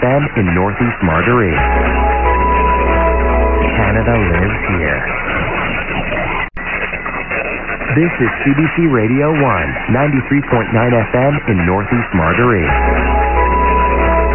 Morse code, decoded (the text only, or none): none